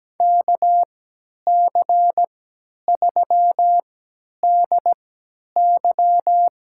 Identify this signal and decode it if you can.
Morse code, heard as a keyed tone, about 17 wpm, spaced as for 10 wpm, 700 Hz. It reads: KC3DY
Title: USA Call Signs - 3x Character Spacing - 17wpm